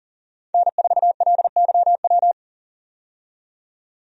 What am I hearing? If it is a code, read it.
Morse code, heard as a keyed tone, about 37 wpm, 700 Hz. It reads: N4LYW
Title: USA Call Signs - 5 Characters - 40wpm